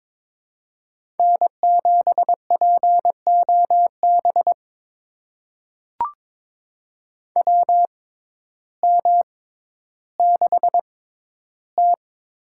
Morse code, heard as a keyed tone, about 22 wpm, spaced as for 12 wpm, 700 Hz. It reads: N7POB WM6T
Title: USA Call Signs - 6x Character Spacing - 22wpm